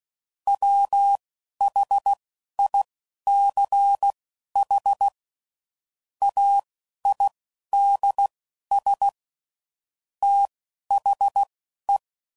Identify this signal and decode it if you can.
Morse code, heard as a keyed tone, about 16 wpm, spaced as for 11 wpm, 800 Hz. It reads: WHICH AIDS THE
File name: CW Code Groups @10WPM / Technical Subjects / XC